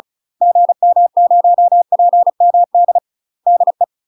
Morse code, heard as a keyed tone, about 36 wpm, 700 Hz. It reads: GM0PMD DE
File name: UK MASTER QSOs TRAINING/35WPM